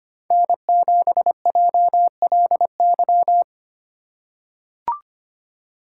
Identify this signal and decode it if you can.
Morse code, heard as a keyed tone, about 25 wpm, 700 Hz. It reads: N7JLY
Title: USA Call Signs - 5 Characters - 2x Character Spacing - 25wpm